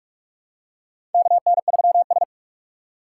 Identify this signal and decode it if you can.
Morse code, heard as a keyed tone, about 44 wpm, 700 Hz. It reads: KN3S
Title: USA Call Signs - 4 Characters - 2x Character Spacing - 45wpm